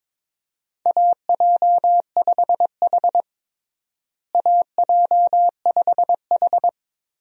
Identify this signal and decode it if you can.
Morse code, heard as a keyed tone, about 22 wpm, 700 Hz. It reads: AJ5H AJ5H
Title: USA Call Signs - 4 Characters - Repeated 2x - 22wpm